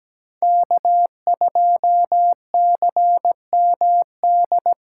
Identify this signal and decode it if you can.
Morse code, heard as a keyed tone, about 18 wpm, 700 Hz. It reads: K2CMD
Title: Speed Racing - USA 1x3 Call Signs 17wpm